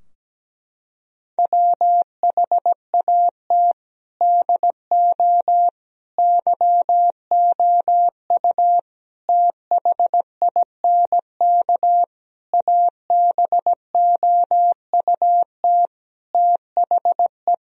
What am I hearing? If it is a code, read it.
Morse code, heard as a keyed tone, about 17 wpm, 700 Hz. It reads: WHAT DO YOU THINK ABOUT THE
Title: Sentences from Top 1000 Words - Review Part 4 - 17wpm